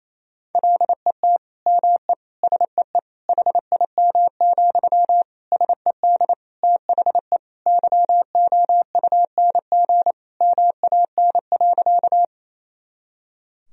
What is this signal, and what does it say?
Morse code, heard as a keyed tone, about 27 wpm, 700 Hz. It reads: LET ME SEE HIM, SED THE YOUNG MAN.
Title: Sentences from Top 400 Words and Abbreviations 28wpm